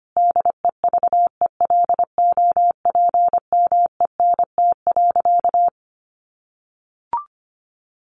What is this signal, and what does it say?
Morse code, heard as a keyed tone, about 25 wpm, 700 Hz. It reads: DEVELOPMENT.
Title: Sentences from Top 1400 Words 25wpm